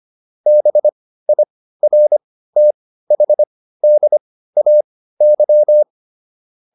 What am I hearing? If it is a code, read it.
Morse code, heard as a keyed tone, about 25 wpm, spaced as for 15 wpm, 600 Hz. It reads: BIRTHDAY